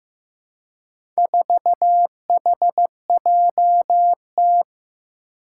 Morse code, heard as a keyed tone, about 15 wpm, 700 Hz. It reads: 4HJT
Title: Four Challenging Letter-Number Combinations 15wpm